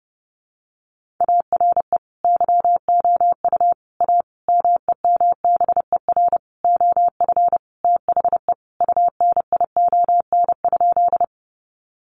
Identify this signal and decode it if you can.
Morse code, heard as a keyed tone, about 30 wpm, 700 Hz. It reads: ARE YOU A MEMBER OF THE UNION?
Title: Sentences from Top 1400 Words 30wpm